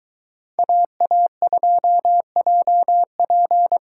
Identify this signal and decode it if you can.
Morse code, heard as a keyed tone, about 23 wpm, 700 Hz. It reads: AA2JP